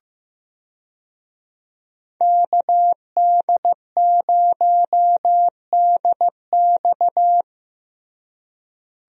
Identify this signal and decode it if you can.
Morse code, heard as a keyed tone, about 15 wpm, 700 Hz. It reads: KD0DX
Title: USA Call Signs - 5 Characters - Repeated 2x - 15wpm